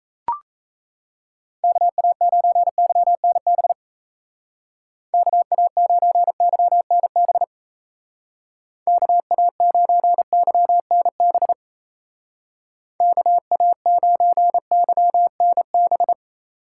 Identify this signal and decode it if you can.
Morse code, heard as a keyed tone, about 35 wpm, 700 Hz. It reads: KA9YNB KA9YNB KA9YNB KA9YNB
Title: Speed Racing - USA 2x3 Call Signs 28wpm